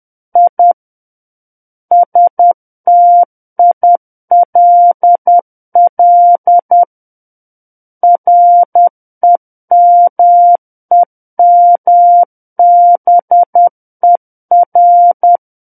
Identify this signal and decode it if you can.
Morse code, heard as a keyed tone, about 10 wpm, 700 Hz. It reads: I STILL REMEMBER
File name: Book One / 01-Chapter 1 - 10 WPM